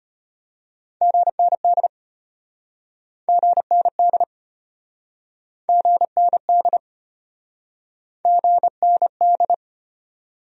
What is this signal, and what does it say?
Morse code, heard as a keyed tone, about 34 wpm, 700 Hz. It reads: GND GND GND GND